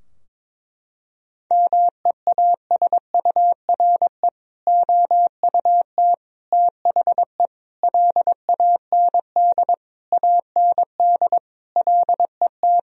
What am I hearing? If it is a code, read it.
Morse code, heard as a keyed tone, about 22 wpm, 700 Hz. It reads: MEASURE OUT THE LAND AND LET
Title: Sentences from Top 1000 Words - Review Part 2 - 22wpm